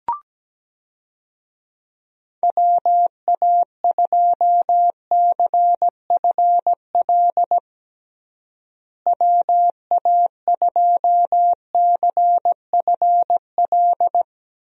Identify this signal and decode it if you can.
Morse code, heard as a keyed tone, about 17 wpm, 700 Hz. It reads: WA2CFL WA2CFL
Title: USA Call Signs - 6 Characters - Repeated 3x - 17wpm